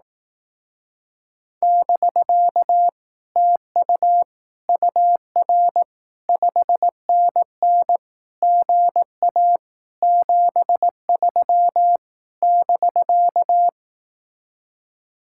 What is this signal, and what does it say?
Morse code, heard as a keyed tone, about 18 wpm, 700 Hz. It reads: <BK> TU UR 5NN GA 73 <BK>
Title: Speed-Racing Common QSO Phrases Encore - 15wpm